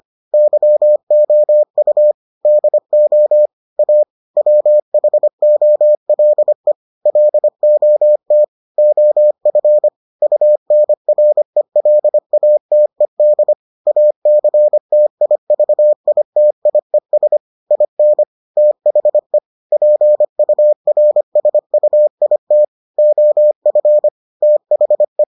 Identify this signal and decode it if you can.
Morse code, heard as a keyed tone, about 25 wpm, 600 Hz. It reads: YOU DO A WHOLE LOT OF UNRELATED ACTIVITIES IN THE PURSUIT OF THE